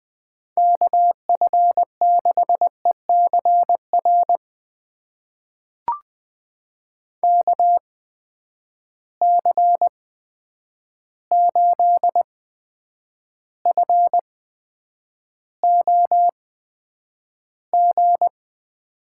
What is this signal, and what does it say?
Morse code, heard as a keyed tone, about 20 wpm, spaced as for 9 wpm, 700 Hz. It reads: KF6ECR KC8FOG
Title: USA Call Signs - 6 Characters - 8x Character Spacing - 20wpm